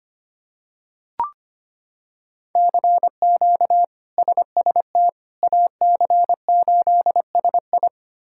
Morse code, heard as a keyed tone, about 25 wpm, 700 Hz. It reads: CQ SST AC8SI
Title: K1USN SST QSO Practice - 25wpm